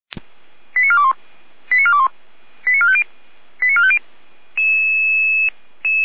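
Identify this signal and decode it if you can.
Five-tone selective call (ZVEI), heard as quick runs of five tones, tones of about 70 ms. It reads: 89421 89421 89450 89450
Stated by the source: ZVEI five-tone call 89315 received by SDR.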